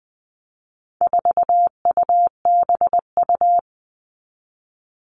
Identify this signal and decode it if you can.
Morse code, heard as a keyed tone, about 20 wpm, 700 Hz. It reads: 4UBU
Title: All Cut Numbers - 4-Digit 20wpm